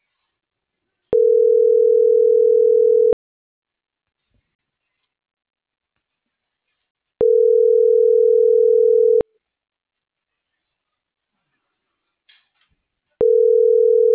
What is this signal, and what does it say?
Ringback tone (North America), 440 + 480 Hz.
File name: CENTRAL MAERSK / out yunuel.morales